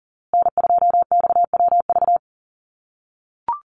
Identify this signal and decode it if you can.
Morse code, heard as a keyed tone, about 40 wpm, 700 Hz. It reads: N2XWV
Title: USA Call Signs Repeated 3x 40wpm